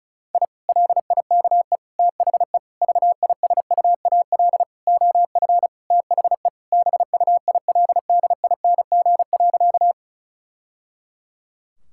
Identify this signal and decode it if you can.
Morse code, heard as a keyed tone, about 34 wpm, 700 Hz. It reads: I LIKE THE VISUAL OF THE BUILDING.
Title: Sentences from Top 1000 Words - Review Part 6 - 35wpm